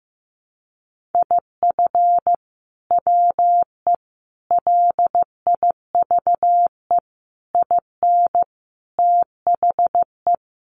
Morse code, heard as a keyed tone, about 15 wpm, 700 Hz. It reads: IF WE LIVE IN THE
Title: Sentences from Top 1400 Words 15wpm